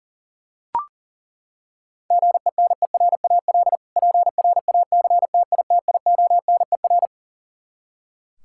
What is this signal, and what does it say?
Morse code, heard as a keyed tone, about 40 wpm, 700 Hz. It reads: GENERAL PRACTITIONER
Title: Practice QSO - Exchange 3 - More Info - 40wpm